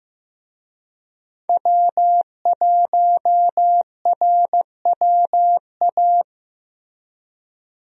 Morse code, heard as a keyed tone, about 15 wpm, 700 Hz. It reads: W1RWA
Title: USA Call Signs - 5 Characters - Repeated 2x - 15wpm